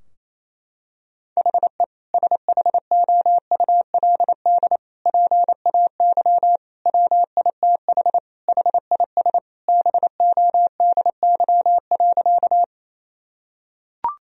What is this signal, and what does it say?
Morse code, heard as a keyed tone, about 28 wpm, 700 Hz. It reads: HE SHOULD PAY WITH HIS BODY.
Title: Sentences from Top 1000 Words - Review Part 2 - 28wpm